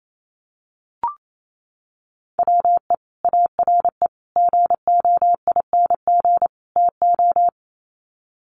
Morse code, heard as a keyed tone, about 28 wpm, 700 Hz. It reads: WE ARE GOING TO
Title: Sets of 4 Words 28wpm